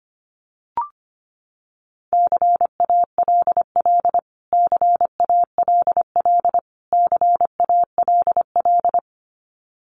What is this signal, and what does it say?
Morse code, heard as a keyed tone, about 25 wpm, 700 Hz. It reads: CALL CALL CALL
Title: Top 200 Words Repeated 25wpm